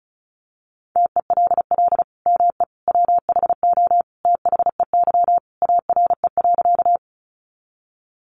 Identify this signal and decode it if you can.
Morse code, heard as a keyed tone, about 34 wpm, 700 Hz. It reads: TELL ME WHO THEY ARE.
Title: Sentences from Top 700 Words 35wpm